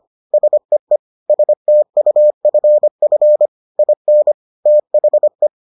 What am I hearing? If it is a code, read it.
Morse code, heard as a keyed tone, about 25 wpm, 600 Hz. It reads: SEE STUFF IN THE